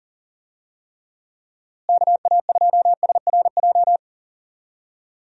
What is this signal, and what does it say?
Morse code, heard as a keyed tone, about 40 wpm, 700 Hz. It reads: KA2SRJ
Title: USA Call Signs - 6 Characters - Repeated 3x - 40wpm